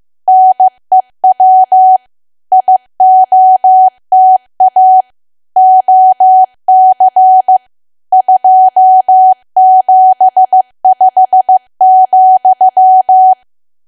Morse code, heard as a keyed tone, about 15 wpm, 750 Hz. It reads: NEW IOTA OC 275,
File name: cwa / W1AW-Level2 / 15WPM